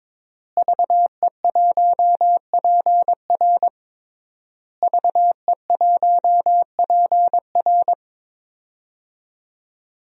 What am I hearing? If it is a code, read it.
Morse code, heard as a keyed tone, about 22 wpm, 700 Hz. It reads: VE1PR VE1PR
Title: DXCC-100 Call Signs - 5 Characters - Repeated 3x - 22wpm